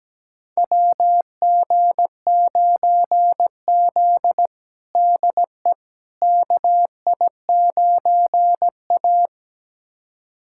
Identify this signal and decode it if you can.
Morse code, heard as a keyed tone, about 17 wpm, 700 Hz. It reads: WG9Z DE KI9A
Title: USA Callsign DE Callsign - 4 Characters - 17wpm